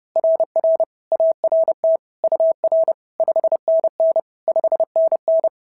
Morse code, heard as a keyed tone, about 30 wpm, 690 Hz. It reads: RR ART UR 5NN 5NN
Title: SOTA QSO Practice 30wpm